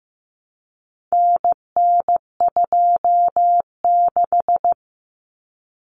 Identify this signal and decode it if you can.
Morse code, heard as a keyed tone, about 15 wpm, 700 Hz. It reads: NN26
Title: Common Cut Numbers - 4-Digit 15wpm